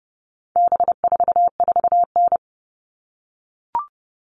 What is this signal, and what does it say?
Morse code, heard as a keyed tone, about 30 wpm, 700 Hz. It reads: B44N
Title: All Cut Numbers - 4-Digit 30wpm